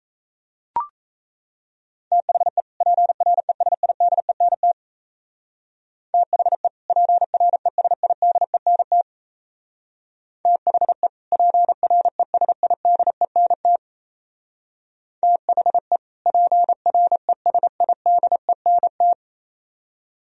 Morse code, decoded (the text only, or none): THE PRESIDENT THE PRESIDENT THE PRESIDENT THE PRESIDENT